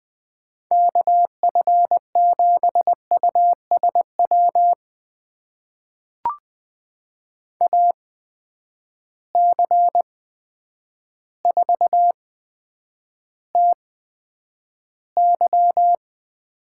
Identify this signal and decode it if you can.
Morse code, heard as a keyed tone, about 20 wpm, spaced as for 9 wpm, 700 Hz. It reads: KF7USW AC4TY
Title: USA Call Signs - 8x Character Spacing - 20wpm